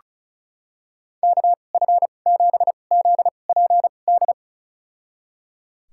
Morse code, heard as a keyed tone, about 36 wpm, spaced as for 25 wpm, 700 Hz. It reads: KF7ZPD
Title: USA Call Signs - 6 Characters - 2x Character Spacing - 35wpm